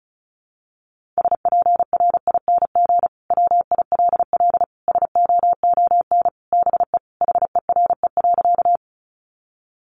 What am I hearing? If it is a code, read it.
Morse code, heard as a keyed tone, about 34 wpm, 700 Hz. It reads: SPRING WILL SOON BE HERE.
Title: Sentences from Top 1200 Words 35wpm